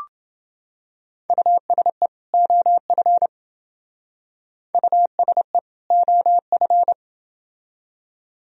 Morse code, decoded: USE OF USE OF

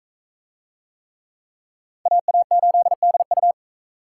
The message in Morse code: AA8DU

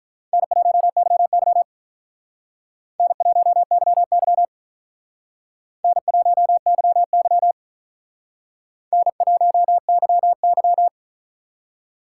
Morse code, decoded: N1YY N1YY N1YY N1YY

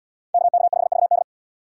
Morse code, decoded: KK5VD

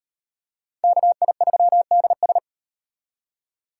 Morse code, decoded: KI3DS

